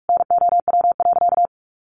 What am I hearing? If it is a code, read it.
Morse code, heard as a keyed tone, about 44 wpm, 700 Hz. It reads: NOW.
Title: Sentences from Top 1300 Words 45wpm